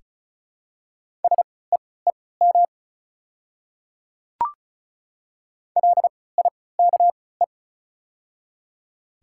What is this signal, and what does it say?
Morse code, heard as a keyed tone, about 34 wpm, spaced as for 20 wpm, 700 Hz. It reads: SEEM LIKE
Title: 3x Character Spacing - Top 300 Words - 35wpm